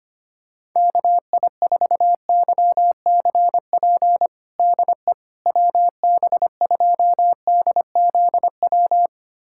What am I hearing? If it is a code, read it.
Morse code, heard as a keyed tone, about 25 wpm, 700 Hz. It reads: KI4YCP DE WB2DZW
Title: USA Callsign DE Callsign - 6 Characters - 25wpm